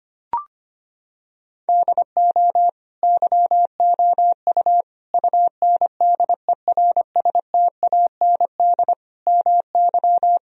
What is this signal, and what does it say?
Morse code, heard as a keyed tone, about 25 wpm, 700 Hz. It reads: DO YOU UNDERSTAND MY
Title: Sentences from Top 1000 Words - Review Part 3 - 25wpm